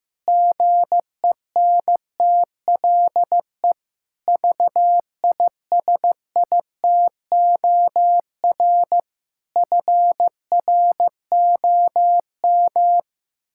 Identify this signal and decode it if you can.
Morse code, heard as a keyed tone, about 15 wpm, 700 Hz. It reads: GENTLE VISITOR FROM
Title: Pangram Challenge v2 - 15wpm